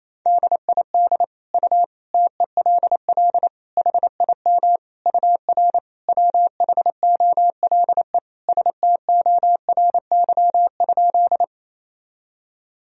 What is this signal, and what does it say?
Morse code, heard as a keyed tone, about 28 wpm, 700 Hz. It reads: DID U TELL HIM UR WHOLE STORY?
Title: Sentences from Top 400 Words and Abbreviations 28wpm